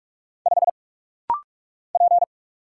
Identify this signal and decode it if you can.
Morse code, heard as a keyed tone, about 44 wpm, 700 Hz. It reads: 5 P
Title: Single Letter-Number - Rapid-Fire 45wpm